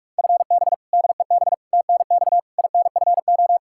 Morse code, heard as a keyed tone, about 43 wpm, 700 Hz. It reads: FB DEB TNX INFO
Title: K1USN SST QSO Practice 45wpm